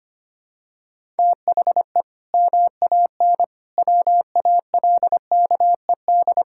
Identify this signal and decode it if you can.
Morse code, heard as a keyed tone, about 25 wpm, 700 Hz. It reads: THE MAN WALKED
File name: practice / 25wpm / Sentences with 2-Word Phrases - 25wpm